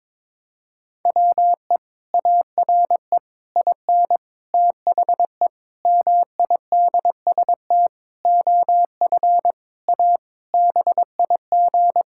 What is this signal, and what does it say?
Morse code, heard as a keyed tone, about 22 wpm, 700 Hz. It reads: WE ARE IN THE MIDST OF A BIG
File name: Sentences with 7-Word Phrases - 22wpm